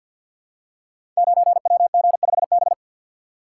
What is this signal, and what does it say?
Morse code, heard as a keyed tone, about 52 wpm, 700 Hz. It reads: 9WG5B
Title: Code Groups - 5 Letter:Numbers - 50wpm